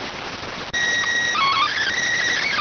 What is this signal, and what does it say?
SSTV (slow-scan television), PD 290.